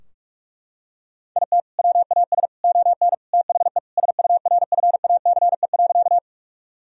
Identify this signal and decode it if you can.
Morse code, heard as a keyed tone, about 42 wpm, 700 Hz. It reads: IT WAS ON THE SURFACE.
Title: Sentences from Top 1000 Words - Review Part 3 - 45wpm